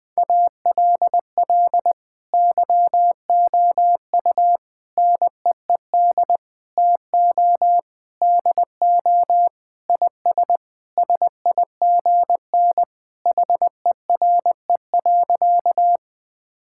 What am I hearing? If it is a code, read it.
Morse code, heard as a keyed tone, about 20 wpm, 700 Hz. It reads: ALL YOU NEED TO DO IS SIGN HERE.
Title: Sentences with 6-Word Phrases - Spoken Beginning - 20wpm